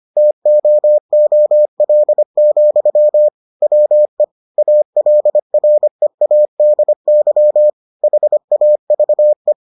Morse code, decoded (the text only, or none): TOOL, WE ALREADY HAVE